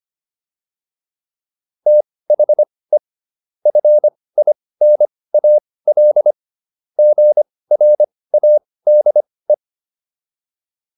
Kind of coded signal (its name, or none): Morse code